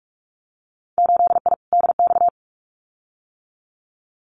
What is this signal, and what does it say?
Morse code, heard as a keyed tone, about 46 wpm, 700 Hz. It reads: 8I DX
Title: Winter Field Day Exchanges Encore 45wpm